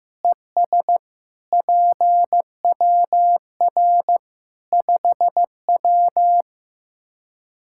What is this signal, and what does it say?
Morse code, heard as a keyed tone, about 15 wpm, 700 Hz. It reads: ES PWR 5W
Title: Practice QSO - Exchange 2 - Info - 15wpm